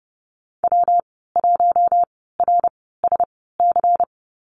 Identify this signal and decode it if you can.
Morse code, heard as a keyed tone, about 30 wpm, spaced as for 17 wpm, 700 Hz. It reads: W1RSC
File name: Lesson 19g - Letter w - Call Signs 30wpm